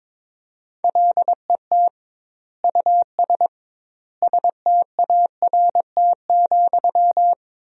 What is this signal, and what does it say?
Morse code, heard as a keyed tone, about 22 wpm, 700 Hz. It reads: LET US START,
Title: 2x Word-Spacing - Sentences from Top 300 Words - 22wpm